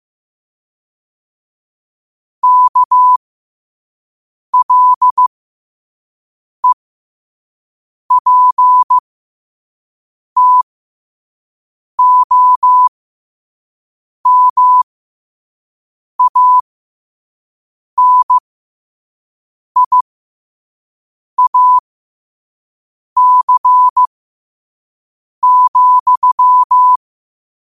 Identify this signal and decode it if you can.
Morse code, heard as a keyed tone, about 15 wpm, spaced as for 5 wpm, 1000 Hz. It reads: KLEPTOMANIAC,
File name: qotdfiles / 5wpm